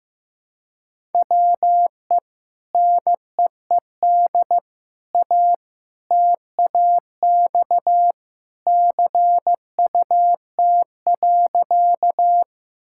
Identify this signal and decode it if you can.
Morse code, heard as a keyed tone, about 15 wpm, 700 Hz. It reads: WE NEED A TAX CUT.
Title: Sentences from Top 1000 Words - Review Part 5 - 15wpm